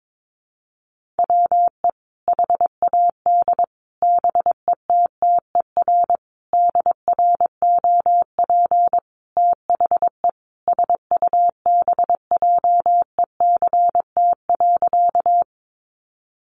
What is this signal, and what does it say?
Morse code, heard as a keyed tone, about 22 wpm, 700 Hz. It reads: WE HAD BETTER DROP THE SUBJECT.
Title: Sentences from Top 1400 Words 22wpm